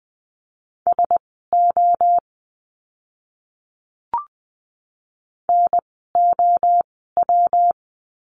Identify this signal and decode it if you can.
Morse code, heard as a keyed tone, about 20 wpm, spaced as for 14 wpm, 700 Hz. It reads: SO NOW
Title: 230 Lesson 37f - Letter x - Words 20wpm